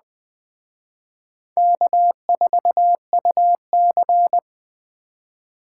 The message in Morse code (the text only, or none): K4UC